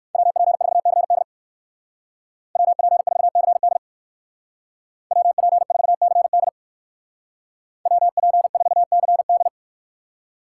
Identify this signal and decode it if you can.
Morse code, heard as a keyed tone, about 59 wpm, 700 Hz. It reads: WP4CD WP4CD WP4CD WP4CD